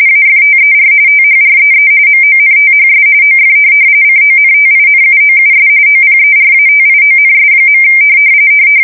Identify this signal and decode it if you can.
RTTY, about 45.45 baud. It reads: AFTER THREE MONTHS OF CONTINUOUS OPERATIONS.
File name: fsk170 / USS-Massachusetts